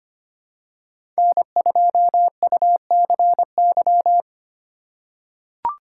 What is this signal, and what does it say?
Morse code, heard as a keyed tone, about 25 wpm, 700 Hz. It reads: N2UCY
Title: USA Call Signs - 5 Characters - 4x Character Spacing - 25wpm